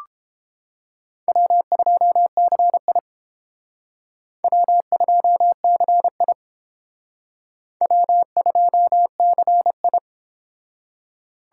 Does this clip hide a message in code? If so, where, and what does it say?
Yes: Morse code from 1.3 to 10.0 s, heard as a keyed tone, about 29 wpm, 700 Hz. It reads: W2CI W2CI W2CI